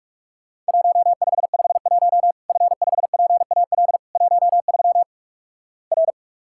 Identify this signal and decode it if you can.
Morse code, heard as a keyed tone, about 46 wpm, 700 Hz. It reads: 1551 F5PAL 13 R
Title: Worked All Europe DX Contest - QTC Exchange - 45wpm